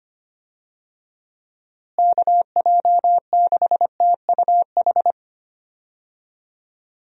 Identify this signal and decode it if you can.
Morse code, heard as a keyed tone, about 25 wpm, 700 Hz. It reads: KJ6TUH
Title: USA Call Signs - 6 Characters - Repeated 3x - 25wpm